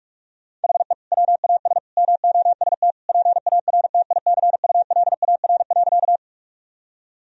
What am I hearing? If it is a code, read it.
Morse code, heard as a keyed tone, about 41 wpm, 700 Hz. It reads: HE WAS MOST PARTICULAR.
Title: Sentences from Top 1000 Words - Review Part 4 - 45wpm